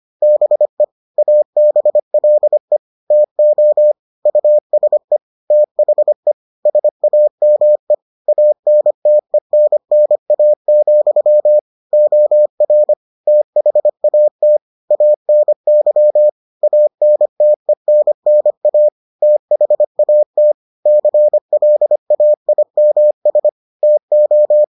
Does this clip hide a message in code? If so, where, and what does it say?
Yes: Morse code from 0.2 to 24.6 s, heard as a keyed tone, about 25 wpm, 600 Hz. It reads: BE ABLE TO USE THE SAME ANTENNA, OR THAT ANY ANTENNA THAT CLAIMS TO